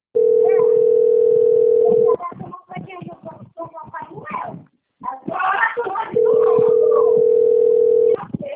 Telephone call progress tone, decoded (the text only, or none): ringback tone